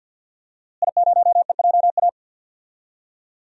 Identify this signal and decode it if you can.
Morse code, heard as a keyed tone, about 50 wpm, 700 Hz. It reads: I0EJA
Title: DXCC-100 Call Signs - 5 Characters - Repeated 3x - 50wpm